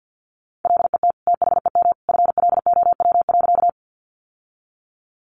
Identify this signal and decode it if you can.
Morse code, heard as a keyed tone, about 50 wpm, 700 Hz. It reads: LET THEM FLOW.
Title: Sentences from Top 1200 Words 50wpm